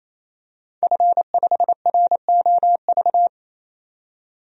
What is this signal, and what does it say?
Morse code, heard as a keyed tone, about 28 wpm, 700 Hz. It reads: F5ROV
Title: DXCC-100 Call Signs - 5 Characters - 28wpm